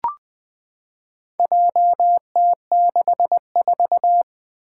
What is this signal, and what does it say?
Morse code, heard as a keyed tone, about 20 wpm, 700 Hz. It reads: JT64
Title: Four Challenging Letter-Number Combinations 20wpm